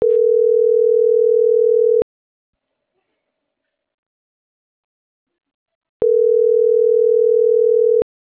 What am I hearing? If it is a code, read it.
Ringback tone (North America), 440 + 480 Hz.